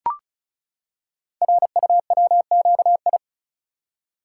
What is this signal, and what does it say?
Morse code, heard as a keyed tone, about 36 wpm, 700 Hz. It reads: RUWQI
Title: Code Groups - 5 Letters - 35wpm